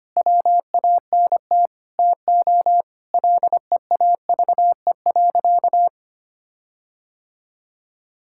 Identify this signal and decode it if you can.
Morse code, heard as a keyed tone, about 25 wpm, 700 Hz. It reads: WANT TO LEAVE.
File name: practice / 25wpm / Sentences with 4-Word Phrases - 25wpm